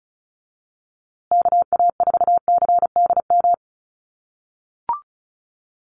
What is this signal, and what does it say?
Morse code, heard as a keyed tone, about 34 wpm, 700 Hz. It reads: KA4CDM
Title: USA Call Signs Repeated 2x 35wpm